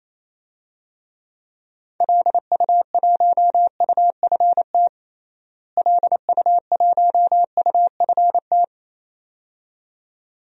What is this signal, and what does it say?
Morse code, heard as a keyed tone, about 28 wpm, 700 Hz. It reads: LU1UFT LU1UFT
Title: DXCC-100 Call Signs - 6 Characters - Repeated 2x - 28wpm